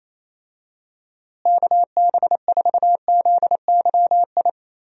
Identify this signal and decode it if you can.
Morse code, heard as a keyed tone, about 28 wpm, 700 Hz. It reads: KB4ZYI
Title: USA Call Signs - 6 Characters - Repeated 2x - 28wpm